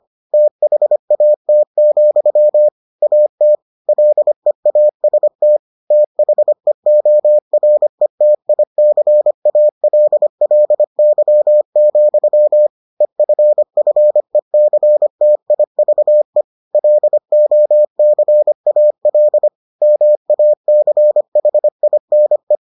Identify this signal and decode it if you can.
Morse code, heard as a keyed tone, about 25 wpm, 600 Hz. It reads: THAT, AT LEAST THEORETICALLY, EFFECTIVE LOCAL MACHINE